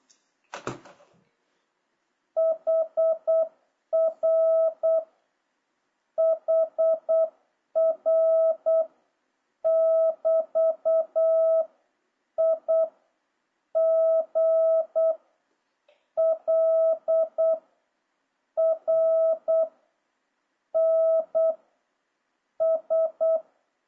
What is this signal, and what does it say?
Morse code, heard as a keyed tone, about 8 wpm, 650 Hz. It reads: HR HR <BT> I G L R N S
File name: jh8cbh / 2026cw